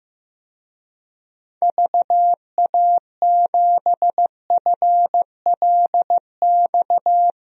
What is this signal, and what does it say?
Morse code, heard as a keyed tone, about 15 wpm, 700 Hz. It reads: VA7FLX
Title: DXCC-100 Call Signs - 6 Characters - Repeated 2x - 15wpm